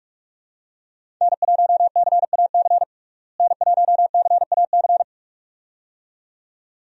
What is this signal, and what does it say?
Morse code, heard as a keyed tone, about 47 wpm, 700 Hz. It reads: N1CAC N1CAC
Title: USA Call Signs - 5 Characters - Repeated 2x - 45wpm